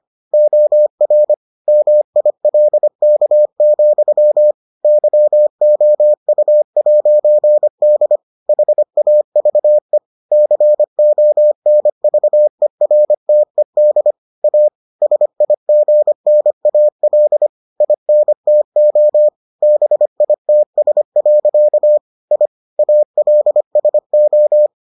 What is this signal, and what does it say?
Morse code, heard as a keyed tone, about 25 wpm, 600 Hz. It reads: OR MILK, YOU'D HAVE CONVERTED A SIGNAL INTO BITS. I ALSO